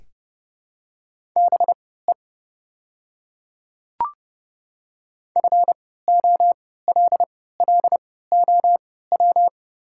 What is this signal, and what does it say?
Morse code, heard as a keyed tone, about 30 wpm, spaced as for 17 wpm, 700 Hz. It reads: BE FOLLOW
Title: 3x Character Spacing - Top 200 Words - 30wpm